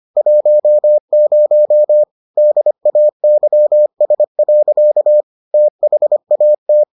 Morse code, heard as a keyed tone, about 25 wpm, 600 Hz. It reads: 10 DAYS. THAT